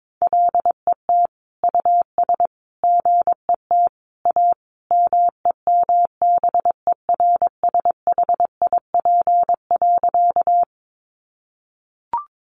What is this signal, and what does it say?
Morse code, heard as a keyed tone, about 22 wpm, 700 Hz. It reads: LET US GET A MEMBERSHIP.
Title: Sentences from Top 1400 Words 22wpm